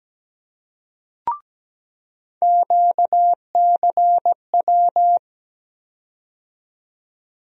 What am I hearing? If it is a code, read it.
Morse code, heard as a keyed tone, about 17 wpm, 700 Hz. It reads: QCW
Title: USA Call Sign Postfixes - 3 Letters 17wpm